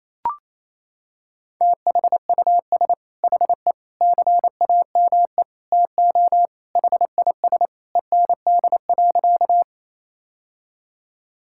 Morse code, heard as a keyed tone, about 28 wpm, 700 Hz. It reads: THUS HE CAME TO HIS END.
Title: Sentences from Top 1000 Words - Review Part 5 - 28wpm